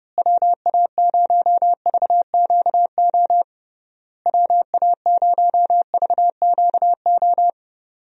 Morse code, heard as a keyed tone, about 32 wpm, 700 Hz. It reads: WA0VQO WA0VQO
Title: USA Call Signs - 6 Characters - Repeated 3x - 30wpm